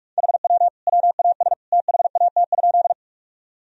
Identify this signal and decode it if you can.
Morse code, heard as a keyed tone, about 42 wpm, 700 Hz. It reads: HW WAS THAT?
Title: Sentences from Top 100 Words and Abbreviations 45wpm